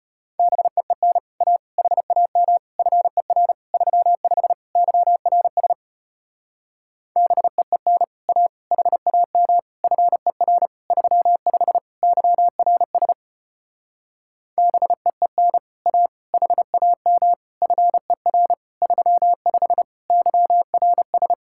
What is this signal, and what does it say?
Morse code, heard as a keyed tone, about 33 wpm, 700 Hz. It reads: BEEN A HAM FER 35 YRS BEEN A HAM FER 35 YRS BEEN A HAM FER 35 YRS